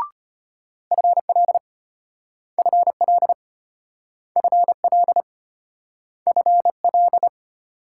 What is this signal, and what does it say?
Morse code, heard as a keyed tone, about 34 wpm, 700 Hz. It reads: FL FL FL FL